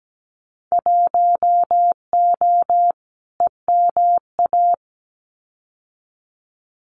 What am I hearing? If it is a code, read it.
Morse code, heard as a keyed tone, about 17 wpm, 700 Hz. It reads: 1O EMA
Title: Winter Field Day Exchanges 17wpm